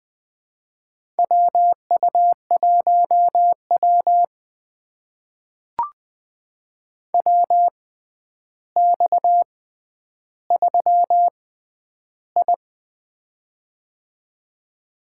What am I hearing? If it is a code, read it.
Morse code, heard as a keyed tone, about 20 wpm, spaced as for 10 wpm, 700 Hz. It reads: WU1W WX3I
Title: USA Call Signs - 4 Characters - 6x Character Spacing - 20wpm